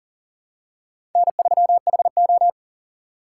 Morse code, heard as a keyed tone, about 40 wpm, 700 Hz. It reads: N3HO
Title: USA Call Signs - 4 Characters - 6x Character Spacing - 40wpm